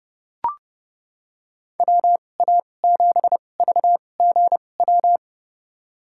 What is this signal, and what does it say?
Morse code, heard as a keyed tone, about 32 wpm, spaced as for 22 wpm, 700 Hz. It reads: WA7VGW
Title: USA Call Signs - 6 Characters - 2x Character Spacing - 30wpm